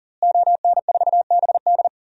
Morse code, heard as a keyed tone, about 40 wpm, 700 Hz. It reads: ON4BD
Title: DXCC-100 Call Signs - 5 Characters - Repeated 2x - 40wpm